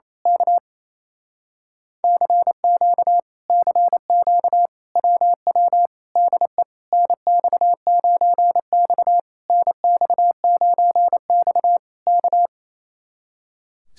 Morse code, heard as a keyed tone, about 26 wpm, 700 Hz. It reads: K CQ CQ WW DE NX9X NX9X K
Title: Speed-Racing Common QSO Phrases - 28wpm